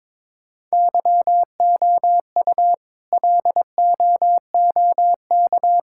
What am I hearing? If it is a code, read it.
Morse code, heard as a keyed tone, about 22 wpm, 700 Hz. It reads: YOU LOOK
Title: Sentences from Top 1000 Words - Review Part 5 - 22wpm